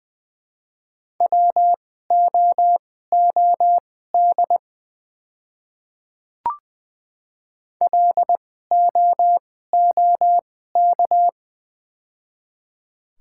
Morse code, heard as a keyed tone, about 20 wpm, spaced as for 15 wpm, 700 Hz. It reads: WOOD LOOK
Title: 2x Character Spacing - Top 300 Words - 20wpm